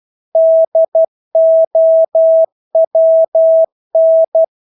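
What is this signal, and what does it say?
Morse code, heard as a keyed tone, about 12 wpm, 650 Hz. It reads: DOWN
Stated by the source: Morse practice of English words at 12 wpm.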